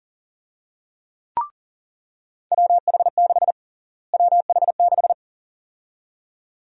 Morse code, heard as a keyed tone, about 37 wpm, 700 Hz. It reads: WH6 WH6
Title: USA Call Sign Prefixes - Repeated 2x - 40wpm